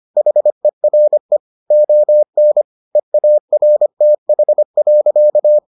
Morse code, heard as a keyed tone, about 25 wpm, 600 Hz. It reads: HERE ON EARTH.